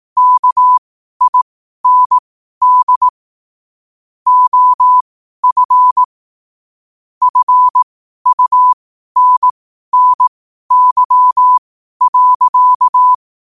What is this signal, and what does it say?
Morse code, heard as a keyed tone, about 18 wpm, spaced as for 12 wpm, 1000 Hz. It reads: KIND OF FUNNY.
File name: qotdfiles / 12wpm